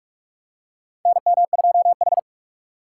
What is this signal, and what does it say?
Morse code, heard as a keyed tone, about 44 wpm, 700 Hz. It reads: NM2H
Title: USA Call Signs - 4 Characters - 6x Character Spacing - 45wpm